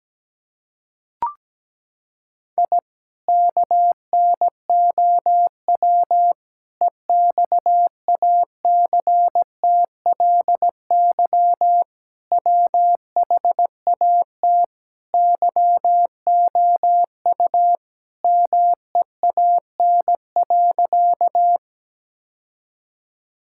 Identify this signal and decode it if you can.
Morse code, heard as a keyed tone, about 17 wpm, 700 Hz. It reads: I KNOW EXACTLY WHAT YOU MEAN.
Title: Sentences from Top 1000 Words - Review Part 6 - 17wpm